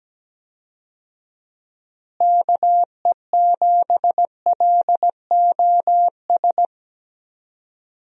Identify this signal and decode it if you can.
Morse code, heard as a keyed tone, about 17 wpm, 700 Hz. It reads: KE7LOS
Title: USA Call Signs - 6 Characters - Repeated 3x - 17wpm